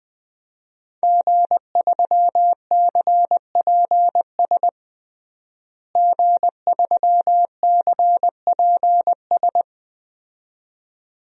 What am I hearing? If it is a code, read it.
Morse code, heard as a keyed tone, about 21 wpm, 700 Hz. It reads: G3CPS G3CPS
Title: DXCC-100 Call Signs - 5 Characters - Repeated 2x - 20wpm